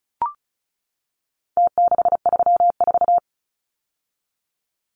Morse code, heard as a keyed tone, about 34 wpm, 700 Hz. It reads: T634
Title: Numbers - Contest Style - 4-Digit 35wpm